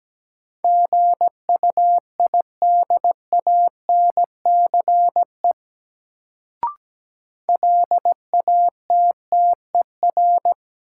Morse code, heard as a keyed tone, about 17 wpm, 700 Hz. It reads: GUIDANCE LATTER
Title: Sentences from Top 1600 Words - New Word Drill - 17wpm